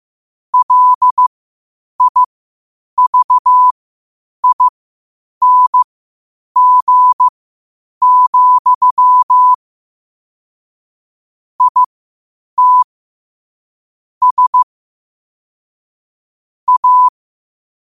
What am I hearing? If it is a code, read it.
Morse code, heard as a keyed tone, about 15 wpm, spaced as for 8 wpm, 1000 Hz. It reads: LIVING, IT S A
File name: qotdfiles / 8wpm